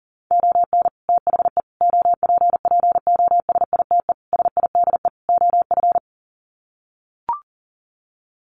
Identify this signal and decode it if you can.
Morse code, heard as a keyed tone, about 40 wpm, 700 Hz. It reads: ON THE OPPOSITE SIDE OF
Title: Sets of 5 Words Encore2 40wpm